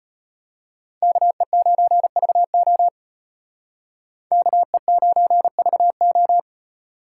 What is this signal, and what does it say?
Morse code, heard as a keyed tone, about 32 wpm, 700 Hz. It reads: KE9VO KE9VO